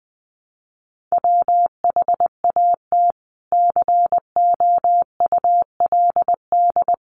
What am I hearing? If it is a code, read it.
Morse code, heard as a keyed tone, about 20 wpm, 700 Hz. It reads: WHAT COULD
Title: Sentences from Top 600 Words 20wpm